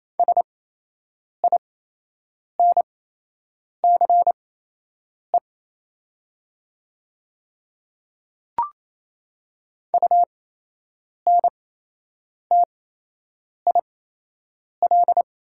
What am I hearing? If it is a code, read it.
Morse code, heard as a keyed tone, about 28 wpm, spaced as for 8 wpm, 700 Hz. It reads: SINCE UNTIL
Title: 8x Character Spacing - Top 300 Words - 28wpm